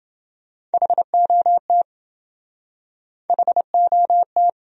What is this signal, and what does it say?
Morse code, heard as a keyed tone, about 27 wpm, 700 Hz. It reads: HOT HOT